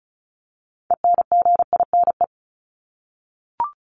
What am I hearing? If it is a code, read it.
Morse code, heard as a keyed tone, about 34 wpm, 700 Hz. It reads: ENGINE
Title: Top 1251-1500 Words 35wpm